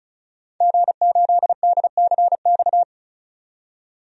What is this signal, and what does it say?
Morse code, heard as a keyed tone, about 36 wpm, 700 Hz. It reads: G8DCX
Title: DXCC-100 Call Signs - 5 Characters - 35wpm